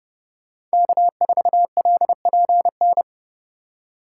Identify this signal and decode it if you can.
Morse code, heard as a keyed tone, about 32 wpm, 700 Hz. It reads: K4LPN
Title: USA Call Signs - 2x Character Spacing - 30wpm